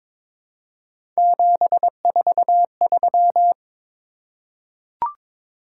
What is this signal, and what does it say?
Morse code, heard as a keyed tone, about 22 wpm, 700 Hz. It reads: 743
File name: practice / 22wpm / Three Letter-Number Combinations - 22wpm